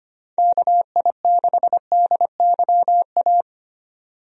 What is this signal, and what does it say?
Morse code, heard as a keyed tone, about 25 wpm, 700 Hz. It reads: KI6DYA
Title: USA Call Signs - 6 Characters - 25wpm